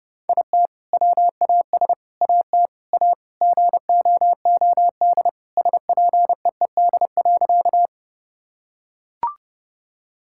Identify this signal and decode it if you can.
Morse code, heard as a keyed tone, about 30 wpm, 700 Hz. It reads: IT WAS AT A GOOD SPEED.
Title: Sentences from Top 1000 Words - Review Part 5 - 30wpm